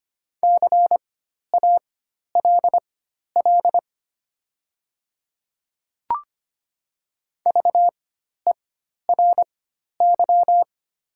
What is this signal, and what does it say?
Morse code, heard as a keyed tone, about 25 wpm, spaced as for 12 wpm, 700 Hz. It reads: CALL VERY
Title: 4x Character Spacing - Top 200 Words - 25wpm